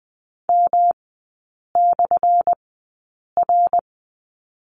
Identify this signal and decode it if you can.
Morse code, heard as a keyed tone, about 20 wpm, spaced as for 8 wpm, 700 Hz. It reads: M/R
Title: 178 Lesson 29e - Slash - Three Characters 20wpm